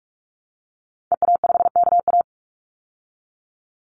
Morse code, heard as a keyed tone, about 44 wpm, 700 Hz. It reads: EA5KA
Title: Call Signs Repeated 2x - 45wpm